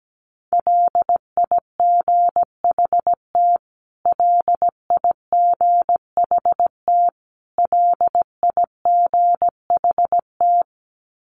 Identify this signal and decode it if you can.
Morse code, heard as a keyed tone, about 17 wpm, 700 Hz. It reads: LIGHT LIGHT LIGHT
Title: Top 300 Words Repeated 17wpm